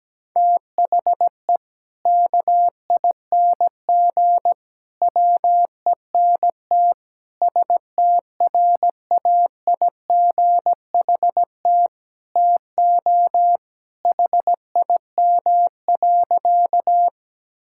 Morse code, heard as a keyed tone, about 17 wpm, 700 Hz. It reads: THE KING WENT STRAIGHT TO HIM.
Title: Sentences from Top 1000 Words - Review Part 6 - 17wpm